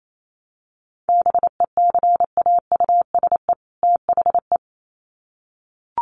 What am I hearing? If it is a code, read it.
Morse code, heard as a keyed tone, about 28 wpm, 700 Hz. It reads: BECAUSE THE
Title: Sets of 2 Words 28wpm